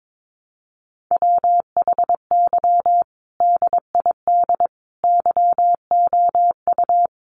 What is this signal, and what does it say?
Morse code, heard as a keyed tone, about 22 wpm, 700 Hz. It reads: WHY DID YOU
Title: Sentences from Top 1300 Words 22wpm